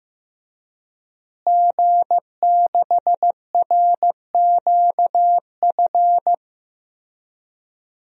Morse code, heard as a keyed tone, about 15 wpm, 700 Hz. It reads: G6RQF
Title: DXCC-100 Call Signs - 5 Characters - Repeated 3x - 15wpm